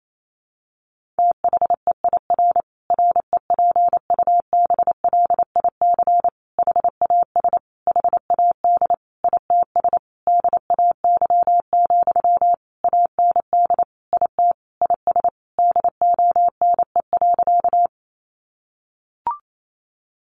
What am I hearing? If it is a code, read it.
Morse code, heard as a keyed tone, about 28 wpm, 700 Hz. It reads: THEIR REPUBLIC HAS HAD ITS DAY, AND IT IS DONE.
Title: Sentences from Top 1500 Words 28wpm